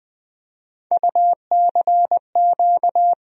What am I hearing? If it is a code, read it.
Morse code, heard as a keyed tone, about 20 wpm, 700 Hz. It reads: UCQ
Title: USA Call Sign Postfixes - 3 Letters 20wpm